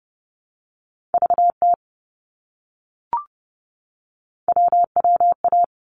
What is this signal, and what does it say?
Morse code, heard as a keyed tone, about 30 wpm, 700 Hz. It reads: VT WWA
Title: ARRL Sections 30wpm